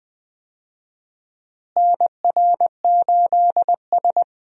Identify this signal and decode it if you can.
Morse code, heard as a keyed tone, about 20 wpm, 700 Hz. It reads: NR8S
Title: USA Call Signs - 4 Characters - Repeated 2x - 20wpm